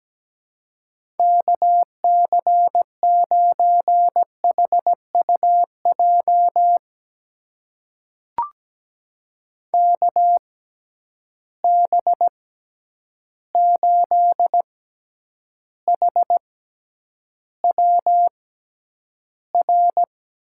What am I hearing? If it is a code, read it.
Morse code, heard as a keyed tone, about 17 wpm, spaced as for 9 wpm, 700 Hz. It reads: KC9HUJ KB8HWR
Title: USA Call Signs - 6 Characters - 6x Character Spacing - 17wpm